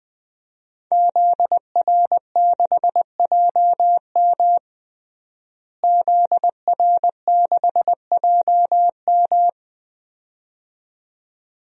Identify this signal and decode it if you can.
Morse code, heard as a keyed tone, about 20 wpm, 700 Hz. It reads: ZR6JM ZR6JM
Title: DXCC-100 Call Signs - 5 Characters - Repeated 2x - 20wpm